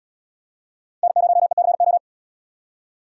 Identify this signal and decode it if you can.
Morse code, heard as a keyed tone, about 78 wpm, 700 Hz. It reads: N0ECO